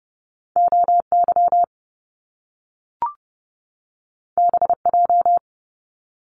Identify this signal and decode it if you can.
Morse code, heard as a keyed tone, about 30 wpm, 700 Hz. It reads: OY BJ
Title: Top Two Letter Combinations in Words 30wpm